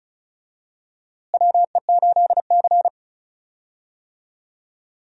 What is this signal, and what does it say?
Morse code, heard as a keyed tone, about 36 wpm, 700 Hz. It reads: WE8C
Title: USA Call Signs - 4 Characters - Repeated 2x - 35wpm